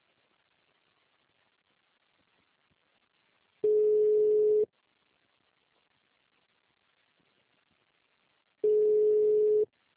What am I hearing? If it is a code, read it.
Ringback tone (Europe, 425 Hz).